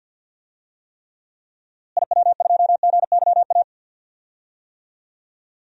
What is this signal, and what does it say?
Morse code, heard as a keyed tone, about 47 wpm, 700 Hz. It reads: IW2GYA
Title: DXCC-100 Call Signs - 6 Characters - Repeated 2x - 50wpm